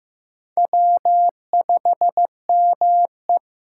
Morse code, heard as a keyed tone, about 15 wpm, 700 Hz. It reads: W5ME
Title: USA Call Signs - 6x Character Spacing - 15wpm